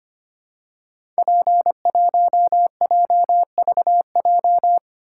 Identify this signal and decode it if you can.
Morse code, heard as a keyed tone, about 25 wpm, 700 Hz. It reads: P1JVJ
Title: Code Groups - 5 Letter:Numbers - 25wpm